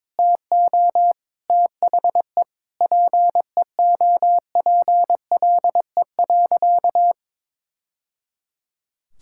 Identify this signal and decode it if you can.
Morse code, heard as a keyed tone, about 22 wpm, 700 Hz. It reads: TO THE PEOPLE.
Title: Sentences from Top 1000 Words - Review Part 6 - 22wpm